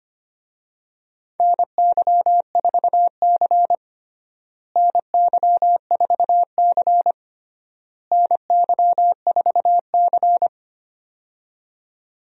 Morse code, heard as a keyed tone, about 25 wpm, 700 Hz. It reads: NY4C NY4C NY4C
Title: USA Call Signs - 4 Characters - Repeated 3x - 25wpm